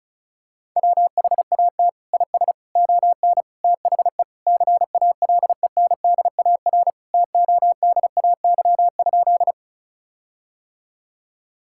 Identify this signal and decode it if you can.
Morse code, heard as a keyed tone, about 36 wpm, 700 Hz. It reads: WHAT IS ON THE CALENDAR TODAY?
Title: Sentences from Top 1600 Words 35wpm